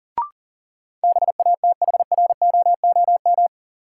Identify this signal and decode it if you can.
Morse code, heard as a keyed tone, about 40 wpm, 700 Hz. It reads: BATHROOM
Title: Sets of 5 Words Encore - New Word Drill - 40wpm